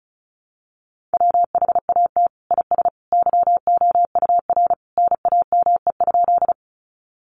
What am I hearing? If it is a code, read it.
Morse code, heard as a keyed tone, about 34 wpm, 700 Hz. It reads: WHAT IS YOUR NAME?
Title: Sentences from Top 200 Words 35wpm